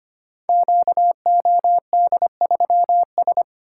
Morse code, heard as a keyed tone, about 25 wpm, 700 Hz. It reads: QOD3S
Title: Code Groups - 5 Letter:Numbers - 25wpm